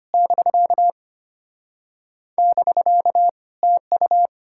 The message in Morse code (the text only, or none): <BK> <BK> TU